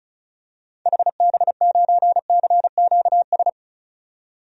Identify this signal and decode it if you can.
Morse code, heard as a keyed tone, about 36 wpm, 700 Hz. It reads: HB9CQS
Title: DXCC-100 Call Signs - 6 Characters - 35wpm